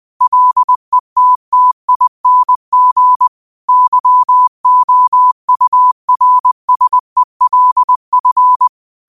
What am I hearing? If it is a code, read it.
Morse code, heard as a keyed tone, about 20 wpm, 1000 Hz. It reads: LETTING YOURSELF
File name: qotdfiles / 20wpm